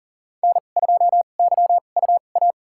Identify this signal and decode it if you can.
Morse code, heard as a keyed tone, about 40 wpm, spaced as for 29 wpm, 700 Hz. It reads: N2YUA
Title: USA Call Signs - 2x Character Spacing - 40wpm